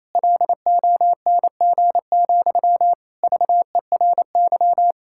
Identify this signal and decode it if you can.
Morse code, heard as a keyed tone, about 28 wpm, 700 Hz. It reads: LONG, VERY
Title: Sentences from Top 1600 Words 28wpm